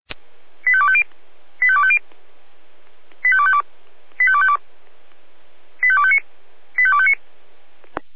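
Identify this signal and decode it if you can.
Five-tone selective call (ZVEI), heard as quick runs of five tones, tones of about 70 ms. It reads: 85250 85250 85252 85252 85259 85259
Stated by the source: ZVEI five-tone selective call received by SDR, logged by Feuerwehr Orlamünde.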